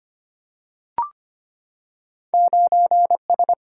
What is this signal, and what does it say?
Morse code, heard as a keyed tone, about 25 wpm, 700 Hz. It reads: 9S
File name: Two Letter-Number Combinations - 25wpm